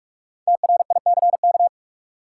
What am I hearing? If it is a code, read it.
Morse code, heard as a keyed tone, about 41 wpm, 700 Hz. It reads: TRICK
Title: Sentences from Top 1500 Words - New Word Drill Alt - 45wpm